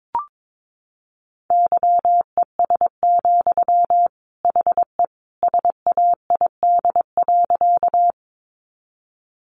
Morse code, heard as a keyed tone, about 22 wpm, 700 Hz. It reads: YES, HE SAID.
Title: Sentences from Top 500 Words 22wpm